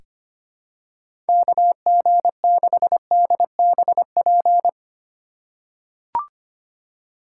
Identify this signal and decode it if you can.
Morse code, heard as a keyed tone, about 25 wpm, 700 Hz. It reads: KG6DBP
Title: USA Call Signs - 6 Characters - 3x Character Spacing - 25wpm